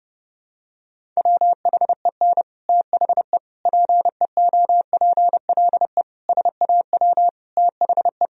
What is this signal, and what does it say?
Morse code, heard as a keyed tone, about 30 wpm, 700 Hz. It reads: WHEN THE PEOPLE SAW THE
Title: Sentences from Top 1000 Words - Review Part 1 - 30wpm